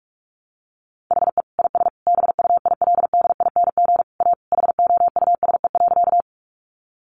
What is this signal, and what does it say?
Morse code, heard as a keyed tone, about 44 wpm, 700 Hz. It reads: HE IS BUILDING A HOUSE.
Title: Sentences from Top 700 Words 45wpm